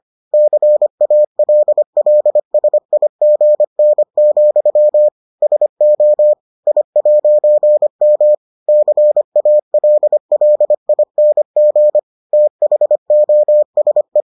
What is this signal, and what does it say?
Morse code, heard as a keyed tone, about 25 wpm, 600 Hz. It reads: CALLSIGN, SO I'M CALLING THOSE